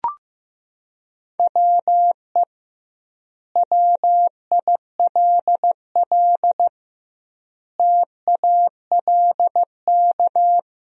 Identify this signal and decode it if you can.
Morse code, heard as a keyed tone, about 15 wpm, 700 Hz. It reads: WE WILL TALK
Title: 2x Word-Spacing - Sentences from Top 400 Words - 15wpm